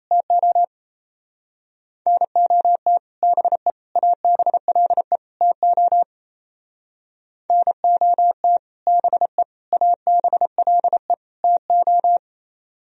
Morse code, heard as a keyed tone, about 33 wpm, 700 Hz. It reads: TO NOT BE ABLE TO NOT BE ABLE TO
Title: Speed-Racing - Sets of 4 Words 28wpm